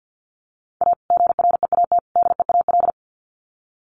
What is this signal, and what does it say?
Morse code, heard as a keyed tone, about 50 wpm, 700 Hz. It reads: A GREAT DEAL
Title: Sets of 3 Words 50wpm